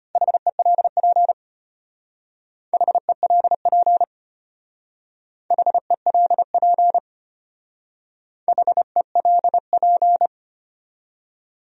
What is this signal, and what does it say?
Morse code, heard as a keyed tone, about 31 wpm, 700 Hz. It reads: HELP HELP HELP HELP
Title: Speed-Racing Top 300 Words 25wpm